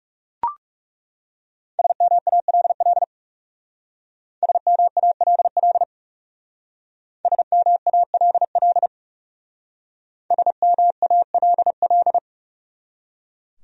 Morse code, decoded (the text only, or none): SMALL SMALL SMALL SMALL